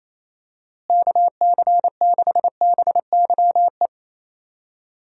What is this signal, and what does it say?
Morse code, heard as a keyed tone, about 28 wpm, 700 Hz. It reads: KC6BYE
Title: USA Call Signs - 6 Characters - Repeated 3x - 28wpm